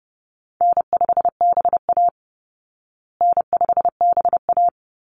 Morse code, heard as a keyed tone, about 30 wpm, 700 Hz. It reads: N5BA N5BA
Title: Call Signs Repeated 3x - 30wpm